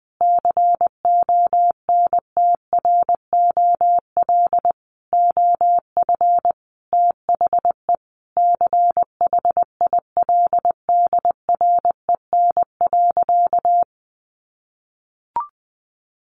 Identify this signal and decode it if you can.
Morse code, heard as a keyed tone, about 20 wpm, 700 Hz. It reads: CONTROL OF THE CHILDREN.
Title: Sentences from Top 1200 Words 20wpm